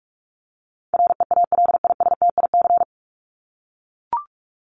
Morse code, heard as a keyed tone, about 44 wpm, 700 Hz. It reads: REALISTIC
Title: Top 1751-2000 Words 45wpm